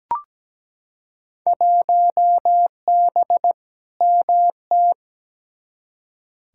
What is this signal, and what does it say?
Morse code, heard as a keyed tone, about 17 wpm, 700 Hz. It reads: 1B MT
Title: Field Day Exchanges - 17wpm